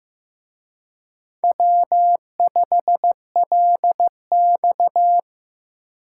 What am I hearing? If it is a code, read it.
Morse code, heard as a keyed tone, about 18 wpm, 700 Hz. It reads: W5LX
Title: Speed Racing - USA 1x2 Call Signs 15wpm